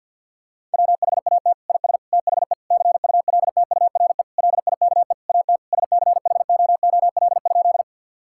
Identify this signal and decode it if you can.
Morse code, heard as a keyed tone, about 50 wpm, 700 Hz. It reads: WHAT IS THE CULTURE LIKE AT SCHOOL?
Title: Sentences from Top 1000 Words - Review Part 6 - 50wpm